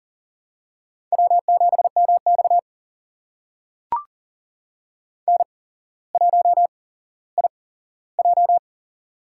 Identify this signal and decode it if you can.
Morse code, heard as a keyed tone, about 39 wpm, spaced as for 17 wpm, 700 Hz. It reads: W7MX N1IJ
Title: USA Call Signs - 4 Characters - 8x Character Spacing - 40wpm